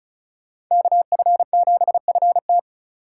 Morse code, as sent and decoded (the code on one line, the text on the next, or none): -.- ..-. --... ..-. -
KF7FT